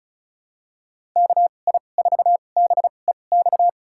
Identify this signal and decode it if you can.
Morse code, heard as a keyed tone, about 34 wpm, spaced as for 25 wpm, 700 Hz. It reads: KI4BEX
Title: USA Call Signs - 6 Characters - 2x Character Spacing - 35wpm